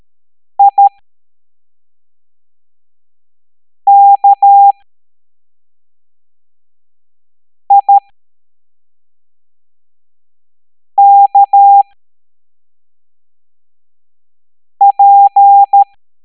Morse code, heard as a keyed tone, about 13 wpm, spaced as for 3 wpm, 800 Hz. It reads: IKIKP
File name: hamdata / L09-EX04